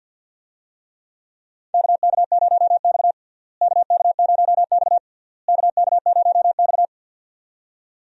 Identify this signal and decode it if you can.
Morse code, heard as a keyed tone, about 48 wpm, 700 Hz. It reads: KK0X KK0X KK0X
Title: USA Call Signs - 4 Characters - Repeated 3x - 50wpm